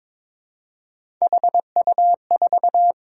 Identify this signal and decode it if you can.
Morse code, heard as a keyed tone, about 22 wpm, 700 Hz. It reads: HU4
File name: practice / 22wpm / Three Letter-Number Combinations - 22wpm